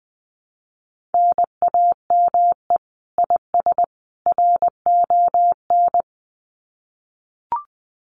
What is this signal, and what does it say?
Morse code, heard as a keyed tone, about 20 wpm, 700 Hz. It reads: NAME IS RON
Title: Common QSO Phrases 20wpm